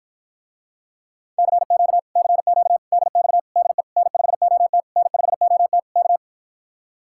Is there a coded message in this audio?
Yes: Morse code from 1.4 to 6.2 s, heard as a keyed tone, about 53 wpm, 700 Hz. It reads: CQ CQ DX DE N5OT N5OT K